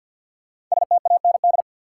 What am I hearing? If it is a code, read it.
Morse code, heard as a keyed tone, about 53 wpm, 700 Hz. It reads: STAND
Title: Sentences from Top 300 Words - New Word Drill - 50wpm